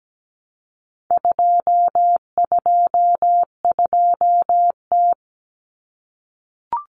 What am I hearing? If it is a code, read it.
Morse code, heard as a keyed tone, about 17 wpm, 700 Hz. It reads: 222T
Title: Common Cut Numbers - 4-Digit 17wpm